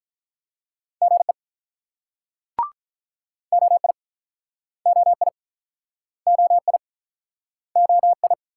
Morse code, heard as a keyed tone, about 47 wpm, 700 Hz. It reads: GE OI OI OI OI